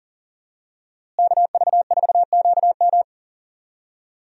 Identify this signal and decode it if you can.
Morse code, heard as a keyed tone, about 37 wpm, 700 Hz. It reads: KV4QM
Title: USA Call Signs - 5 Characters - Repeated 2x - 40wpm